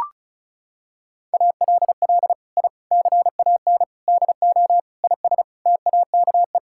Morse code, heard as a keyed tone, about 36 wpm, 700 Hz. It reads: ALL I CAN DO IS TAKE
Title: Sentences from Top 1000 Words - Review Part 6 - 35wpm